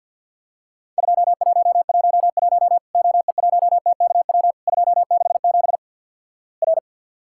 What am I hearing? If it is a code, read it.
Morse code, heard as a keyed tone, about 48 wpm, 700 Hz. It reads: 2111 OE1TKW 267 R
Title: Worked All Europe DX Contest - QTC Exchange - 50wpm